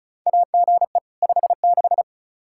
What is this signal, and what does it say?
Morse code, heard as a keyed tone, about 36 wpm, 700 Hz. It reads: AGE 56
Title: K1USN SST QSO Practice 35wpm